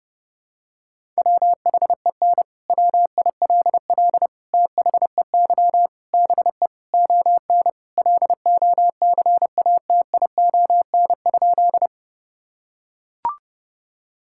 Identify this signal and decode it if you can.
Morse code, heard as a keyed tone, about 32 wpm, 700 Hz. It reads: WHEN WILL THEY BE ON LOCATION?
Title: Sentences from Top 1000 Words - Review Part 6 - 30wpm